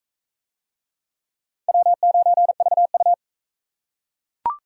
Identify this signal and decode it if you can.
Morse code, heard as a keyed tone, about 39 wpm, 700 Hz. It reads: W9VU